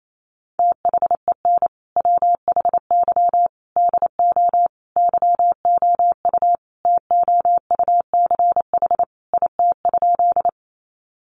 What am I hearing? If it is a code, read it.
Morse code, heard as a keyed tone, about 28 wpm, 700 Hz. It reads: THEN WHY DO YOU TOUCH IT?
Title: Sentences from Top 1200 Words 28wpm